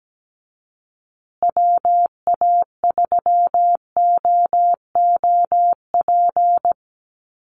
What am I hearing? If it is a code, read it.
Morse code, heard as a keyed tone, about 17 wpm, 700 Hz. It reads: WA3OOP
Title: USA Call Signs Repeated 3x 17wpm